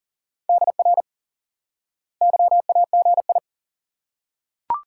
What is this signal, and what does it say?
Morse code, heard as a keyed tone, about 37 wpm, 700 Hz. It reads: DR YAGI
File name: course / instant-qso-element-course-3 / IQE Course 3 - Lesson 89 - YAGI - Two Elements - 40wpm